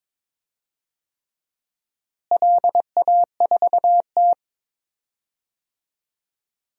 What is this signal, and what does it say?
Morse code, heard as a keyed tone, about 22 wpm, 700 Hz. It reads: LA4T
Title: DXCC-100 Call Signs - 4 Characters - Repeated 3x - 22wpm